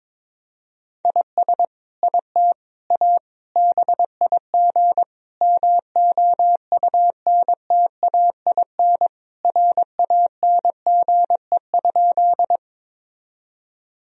Morse code, heard as a keyed tone, about 22 wpm, 700 Hz. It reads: IS IT A BIG MOUNTAIN RANGE?
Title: Sentences from Top 1000 Words - Review Part 4 - 22wpm